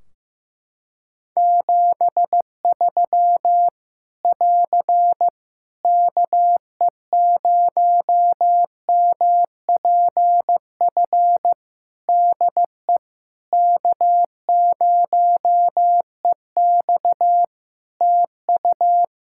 Morse code, heard as a keyed tone, about 15 wpm, 700 Hz. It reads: 73 <AR> KE0MPF DE K0EX TU
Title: Practice QSO - Exchange 4 - Ending - 15wpm